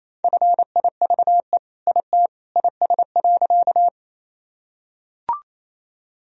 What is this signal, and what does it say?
Morse code, heard as a keyed tone, about 27 wpm, 700 Hz. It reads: FIVE IT IS.
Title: Sentences from Top 1000 Words - Review Part 2 - 28wpm